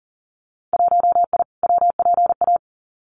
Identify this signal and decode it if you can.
Morse code, heard as a keyed tone, about 40 wpm, 700 Hz. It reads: 1I WPA
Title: Winter Field Day Exchanges Encore 40wpm